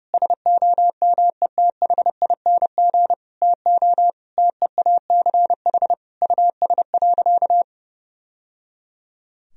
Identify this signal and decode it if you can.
Morse code, heard as a keyed tone, about 32 wpm, 700 Hz. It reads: SOMETHING TO TEACH US.
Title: Sentences from Top 1000 Words - Review Part 2 - 30wpm